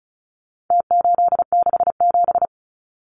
Morse code, heard as a keyed tone, about 34 wpm, 700 Hz. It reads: T867
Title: Numbers - Contest Style - 4-Digit 35wpm